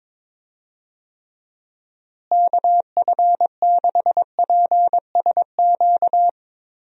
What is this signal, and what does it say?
Morse code, heard as a keyed tone, about 21 wpm, 700 Hz. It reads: KF6PSQ
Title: USA Call Signs - 6 Characters - Repeated 2x - 22wpm